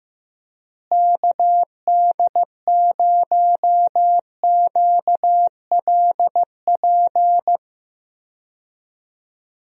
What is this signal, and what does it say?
Morse code, heard as a keyed tone, about 15 wpm, 700 Hz. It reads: KD0QLP
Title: USA Call Signs - 6 Characters - Repeated 3x - 15wpm